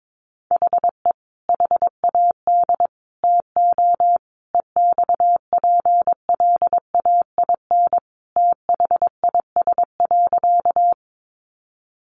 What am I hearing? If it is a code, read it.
Morse code, heard as a keyed tone, about 22 wpm, 700 Hz. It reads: HE HAD TO EXPLAIN THIS.
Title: Sentences from Top 1100 Words 22wpm